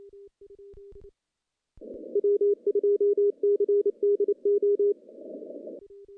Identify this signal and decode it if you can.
Morse code, heard as a keyed tone, about 27 wpm, 410 Hz. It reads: W2CDO
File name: contests / CW-SS-k5tr- -1400